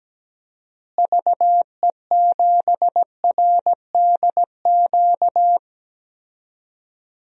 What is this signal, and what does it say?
Morse code, heard as a keyed tone, about 17 wpm, 700 Hz. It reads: VE7RDQ
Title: DXCC-100 Call Signs - 6 Characters - Repeated 3x - 17wpm